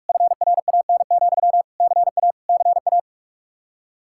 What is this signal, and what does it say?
Morse code, heard as a keyed tone, about 56 wpm, 700 Hz. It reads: FRAN, CA CA
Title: Speed-Racing Common QSO Phrases - 45wpm